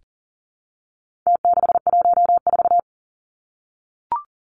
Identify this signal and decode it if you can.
Morse code, heard as a keyed tone, about 40 wpm, 700 Hz. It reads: T614
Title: Numbers - Contest Style - 4-Digit 40wpm